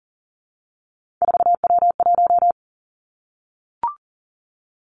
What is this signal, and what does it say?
Morse code, heard as a keyed tone, about 40 wpm, 700 Hz. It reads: 4W1
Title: Three Challenging Letter-Number Combinations 40wpm